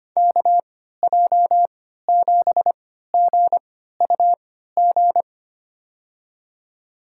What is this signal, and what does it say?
Morse code, heard as a keyed tone, about 25 wpm, spaced as for 14 wpm, 700 Hz. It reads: KJ7GUG
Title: USA Call Signs - 6 Characters - 3x Character Spacing - 25wpm